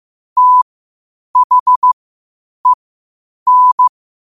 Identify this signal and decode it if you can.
Morse code, heard as a keyed tone, about 15 wpm, spaced as for 8 wpm, 1000 Hz. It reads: THEN
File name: qotdfiles / 8wpm